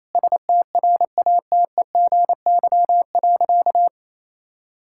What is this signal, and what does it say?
Morse code, heard as a keyed tone, about 28 wpm, 700 Hz. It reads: STRATEGY.
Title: Sentences from Top 1000 Words - Review Part 5 - 28wpm